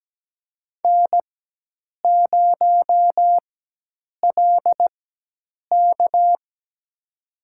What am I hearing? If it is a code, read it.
Morse code, heard as a keyed tone, about 17 wpm, spaced as for 8 wpm, 700 Hz. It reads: N0LK
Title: USA Call Signs - 4x Character Spacing - 17wpm